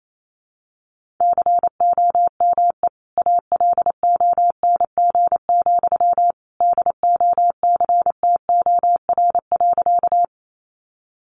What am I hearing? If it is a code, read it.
Morse code, heard as a keyed tone, about 28 wpm, 700 Hz. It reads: COME ALONG, DOCTOR.
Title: Sentences from Top 1500 Words 28wpm